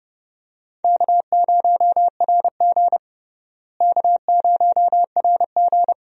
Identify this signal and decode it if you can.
Morse code, heard as a keyed tone, about 32 wpm, 700 Hz. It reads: K0RG K0RG
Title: USA Call Signs - 4 Characters - Repeated 2x - 30wpm